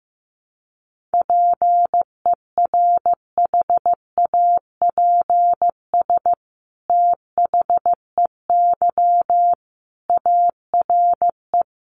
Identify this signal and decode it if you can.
Morse code, heard as a keyed tone, about 15 wpm, 700 Hz. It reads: PERHAPS THEY ARE
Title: Sentences from Top 700 Words 15wpm